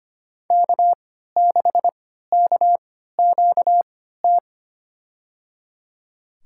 Morse code, heard as a keyed tone, about 25 wpm, spaced as for 14 wpm, 700 Hz. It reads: K6KQT
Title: USA Call Signs - 5 Characters - 3x Character Spacing - 25wpm